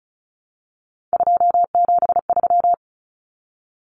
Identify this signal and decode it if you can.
Morse code, heard as a keyed tone, about 34 wpm, 700 Hz. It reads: 273
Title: Common Cut Numbers - 3-Digit 35wpm